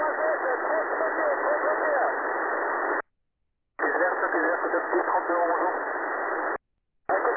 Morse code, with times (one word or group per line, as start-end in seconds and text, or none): none